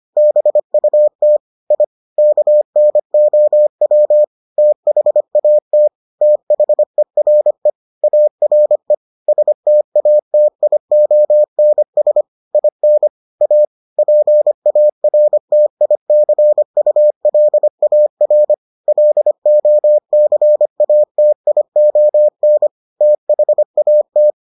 Morse code, heard as a keyed tone, about 25 wpm, 600 Hz. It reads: BUT I KNOW THAT THERE ARE STATIONS IN A PARTICULAR LOCATION THAT